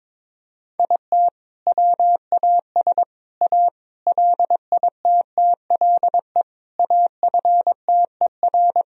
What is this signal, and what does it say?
Morse code, heard as a keyed tone, about 22 wpm, 700 Hz. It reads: IT WAS A LITTLE AFTER
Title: Sentences from Top 1000 Words - Review Part 3 - 22wpm